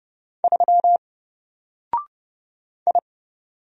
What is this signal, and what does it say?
Morse code, heard as a keyed tone, about 29 wpm, 700 Hz. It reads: 3 I
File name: Single-Letter-Number-30wpm